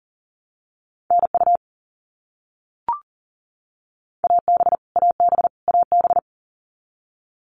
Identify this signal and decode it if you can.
Morse code, heard as a keyed tone, about 40 wpm, 700 Hz. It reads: NU AB AB AB
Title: Canadian Provinces and Territories - Repeated 3x - 40wpm